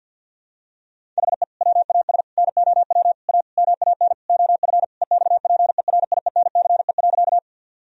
Morse code, heard as a keyed tone, about 50 wpm, 700 Hz. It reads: HE WAS NOW A MAN OF EXPERIENCE.
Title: Sentences from Top 1000 Words - Review Part 3 - 50wpm